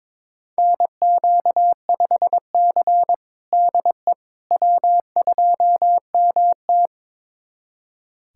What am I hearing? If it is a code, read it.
Morse code, heard as a keyed tone, about 22 wpm, 700 Hz. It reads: NQ5C DE W2MT
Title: USA Callsign DE Callsign - 4 Characters - 22wpm